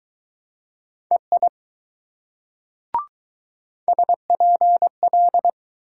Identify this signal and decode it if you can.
Morse code, heard as a keyed tone, about 19 wpm, 700 Hz. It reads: EI SPL